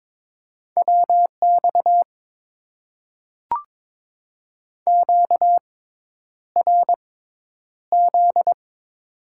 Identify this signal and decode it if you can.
Morse code, heard as a keyed tone, about 22 wpm, spaced as for 10 wpm, 700 Hz. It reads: WX QRZ
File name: Top 90 QSO Elements - 6x Character Spacing - 22wpm